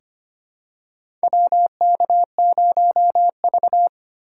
Morse code, heard as a keyed tone, about 25 wpm, 690 Hz. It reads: WK0V
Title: USA Call Signs - 4 Characters - 4x Character Spacing - 25wpm